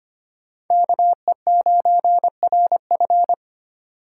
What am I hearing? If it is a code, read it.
Morse code, heard as a keyed tone, about 25 wpm, 700 Hz. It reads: KE9RF
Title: USA Call Signs - 5 Characters - 2x Character Spacing - 25wpm